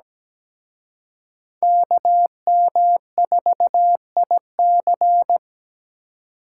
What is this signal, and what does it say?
Morse code, heard as a keyed tone, about 20 wpm, 700 Hz. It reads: KM4IC
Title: Speed Racing - USA Call Signs 17wpm